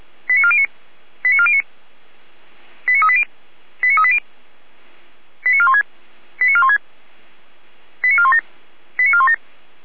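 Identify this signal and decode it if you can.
Five-tone selective call (ZVEI), heard as quick runs of five tones, tones of about 70 ms. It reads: 89409 89409 89380 89380 89416 89416 89417 89417
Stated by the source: ZVEI five-tone call 89500 received by SDR, fire brigade Feuerwehr Orlamünde.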